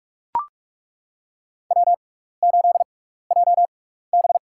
Morse code, heard as a keyed tone, about 41 wpm, spaced as for 15 wpm, 700 Hz. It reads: W8JB
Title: USA Call Signs - 4 Characters - 6x Character Spacing - 45wpm